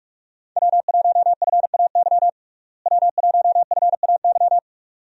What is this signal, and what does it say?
Morse code, heard as a keyed tone, about 46 wpm, 700 Hz. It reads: W1FAY W1FAY
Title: USA Call Signs - 5 Characters - Repeated 2x - 45wpm